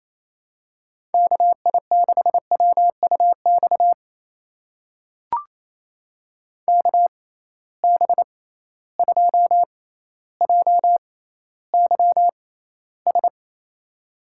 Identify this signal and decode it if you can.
Morse code, heard as a keyed tone, about 28 wpm, spaced as for 14 wpm, 700 Hz. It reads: KI6WUX KB2JYS
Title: USA Call Signs - 6 Characters - 6x Character Spacing - 28wpm